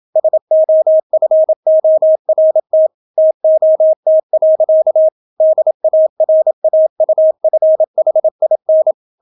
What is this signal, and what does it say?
Morse code, heard as a keyed tone, about 27 wpm, 630 Hz. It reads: SOFORT TOT. DARAUFHIN